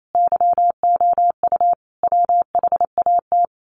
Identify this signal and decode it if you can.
Morse code, heard as a keyed tone, about 28 wpm, 700 Hz. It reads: YOU WHAT
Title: Sets of 5 Words Encore 28wpm